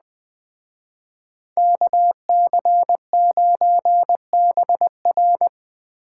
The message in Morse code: KC9BR

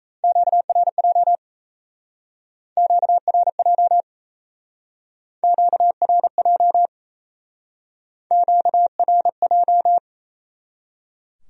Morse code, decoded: QRJ QRJ QRJ QRJ

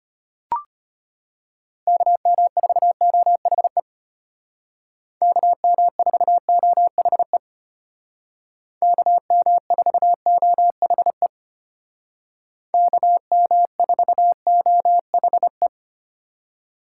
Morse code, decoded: KM4OHE KM4OHE KM4OHE KM4OHE

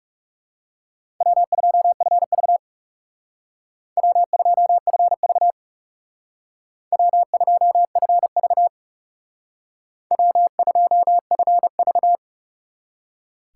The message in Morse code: W2FV W2FV W2FV W2FV

